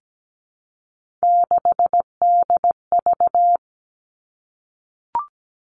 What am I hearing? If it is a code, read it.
Morse code, heard as a keyed tone, about 17 wpm, 700 Hz. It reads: 6DV
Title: All Cut Numbers - 3-Digit 17wpm